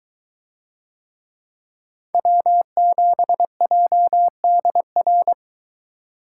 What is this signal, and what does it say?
Morse code, heard as a keyed tone, about 21 wpm, 700 Hz. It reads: W7JDR